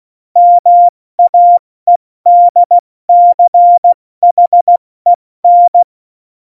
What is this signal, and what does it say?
Morse code, heard as a keyed tone, about 16 wpm, 700 Hz. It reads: MAEDCHEN
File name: Morsekurs / Witze / Witz 11 August 2020, 70 F80